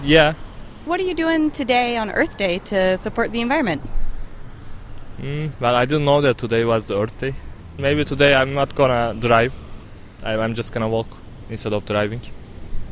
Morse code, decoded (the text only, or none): none